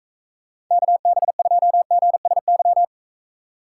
Morse code, heard as a keyed tone, about 44 wpm, 700 Hz. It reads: KB2GSY